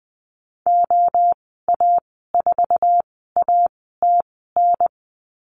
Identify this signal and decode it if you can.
Morse code, heard as a keyed tone, about 20 wpm, spaced as for 15 wpm, 700 Hz. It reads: OA4ATN
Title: 055 Lesson 10g - Letter r - Call Signs 20wpm